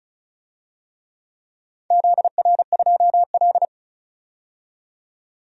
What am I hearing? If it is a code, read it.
Morse code, heard as a keyed tone, about 36 wpm, 700 Hz. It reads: ZR2L
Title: DXCC-100 Call Signs - 4 Characters - 35wpm